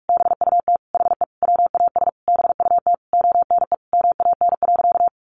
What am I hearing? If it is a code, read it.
Morse code, heard as a keyed tone, about 44 wpm, 700 Hz. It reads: BUT HE WAS BUT ONE MAN.
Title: Sentences from Top 1500 Words 45wpm